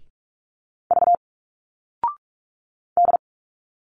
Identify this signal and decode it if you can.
Morse code, heard as a keyed tone, about 45 wpm, 700 Hz. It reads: V D
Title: Single Letters 45wpm